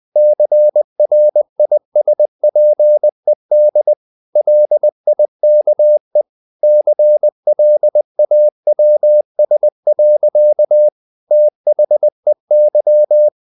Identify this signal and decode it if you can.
Morse code, heard as a keyed tone, about 20 wpm, 600 Hz. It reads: CRISPED LIKE CLAWS. THEY
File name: the old man and the sea part 2 20wpm